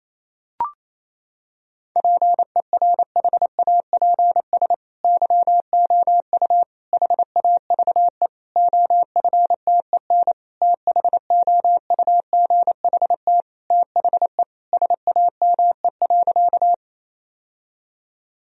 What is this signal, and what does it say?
Morse code, heard as a keyed tone, about 28 wpm, 700 Hz. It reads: PERHAPS YOU HAVE OFTEN THOUGHT THE SAME.
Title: Sentences from Top 1000 Words - Review Part 1 - 28wpm